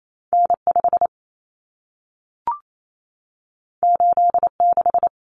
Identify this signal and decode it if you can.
Morse code, heard as a keyed tone, about 28 wpm, 700 Hz. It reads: N5 86
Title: Common Cut Numbers - 2-Digit 28wpm